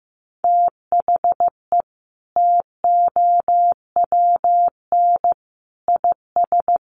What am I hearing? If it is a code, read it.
Morse code, heard as a keyed tone, about 15 wpm, 700 Hz. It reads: THE TOWN IS
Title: Sentences from Top 1300 Words 15wpm